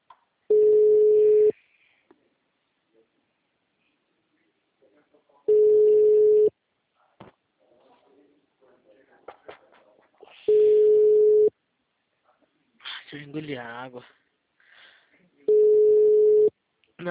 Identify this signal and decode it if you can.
Ringback tone (Europe, 425 Hz), with speech.
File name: out- -2015- - - .92291